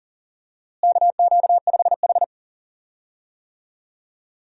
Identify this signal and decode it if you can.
Morse code, heard as a keyed tone, about 37 wpm, 700 Hz. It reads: KQ5H
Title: USA Call Signs - 4 Characters - Repeated 3x - 40wpm